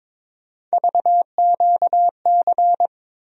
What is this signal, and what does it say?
Morse code, heard as a keyed tone, about 21 wpm, 700 Hz. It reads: VQC
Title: USA Call Sign Postfixes - 3 Letters 22wpm